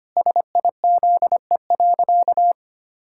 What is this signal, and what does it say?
Morse code, heard as a keyed tone, about 25 wpm, 700 Hz. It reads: SIZE.
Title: Pangram Challenge v2 - 25wpm